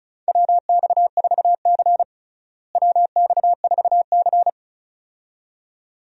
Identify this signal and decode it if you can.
Morse code, heard as a keyed tone, about 34 wpm, 700 Hz. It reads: WX4C WX4C
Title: USA Call Signs - 4 Characters - Repeated 2x - 35wpm